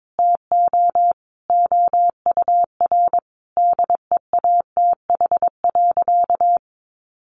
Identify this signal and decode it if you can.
Morse code, heard as a keyed tone, about 22 wpm, 700 Hz. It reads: TO OUR DEATH.
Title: Sentences from Top 1200 Words 22wpm